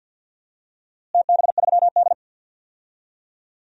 Morse code, heard as a keyed tone, about 52 wpm, 700 Hz. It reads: TB3D
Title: DXCC-100 Call Signs - 4 Characters - Repeated 2x - 50wpm